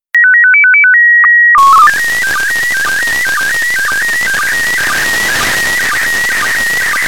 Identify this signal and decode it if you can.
SSTV (slow-scan television), PD 120.
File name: ariss-sstv / ariss- -1620